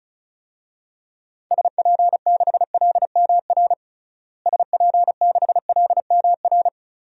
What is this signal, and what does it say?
Morse code, heard as a keyed tone, about 34 wpm, 700 Hz. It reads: SP6LMR SP6LMR
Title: DXCC-100 Call Signs - 6 Characters - Repeated 2x - 35wpm